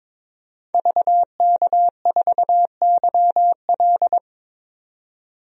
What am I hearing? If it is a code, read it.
Morse code, heard as a keyed tone, about 22 wpm, 700 Hz. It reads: VK4YL
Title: DXCC-100 Call Signs - 5 Characters - 22wpm